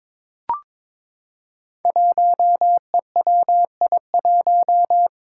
Morse code, heard as a keyed tone, about 22 wpm, 700 Hz. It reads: 1EWI1
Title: Five Challenging Letter-Number Combinations 22wpm